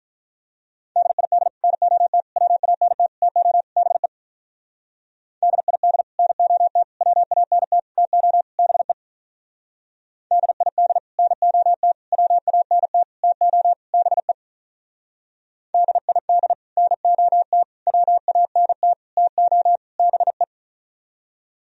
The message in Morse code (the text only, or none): DID NOT WANT TO BE DID NOT WANT TO BE DID NOT WANT TO BE DID NOT WANT TO BE